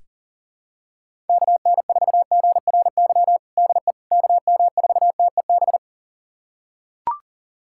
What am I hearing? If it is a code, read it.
Morse code, heard as a keyed tone, about 40 wpm, 700 Hz. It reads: KN4GRY DE KM4TEB
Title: USA Callsign DE Callsign 40wpm